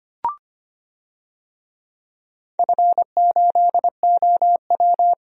Morse code, heard as a keyed tone, about 25 wpm, 700 Hz. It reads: F8OW
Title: DXCC-100 Call Signs - 4 Characters - Repeated 3x - 25wpm